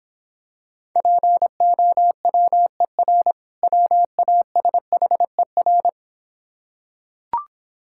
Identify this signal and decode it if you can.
Morse code, heard as a keyed tone, about 26 wpm, 690 Hz. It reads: POWER WASHER